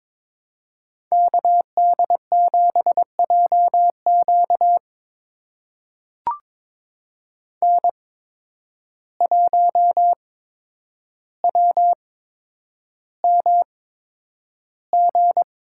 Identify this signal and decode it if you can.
Morse code, heard as a keyed tone, about 22 wpm, spaced as for 9 wpm, 700 Hz. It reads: KD7JQ N1WMG
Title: USA Call Signs - 5 Characters - 8x Character Spacing - 22wpm